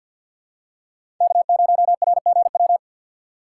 Morse code, heard as a keyed tone, about 51 wpm, 690 Hz. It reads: K0RGW
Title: USA Call Signs - 5 Characters - 3x Character Spacing - 50wpm